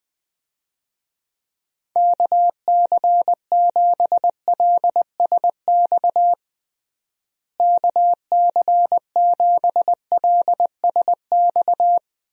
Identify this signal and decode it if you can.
Morse code, heard as a keyed tone, about 21 wpm, 700 Hz. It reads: KC7LSX KC7LSX
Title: USA Call Signs - 6 Characters - Repeated 3x - 20wpm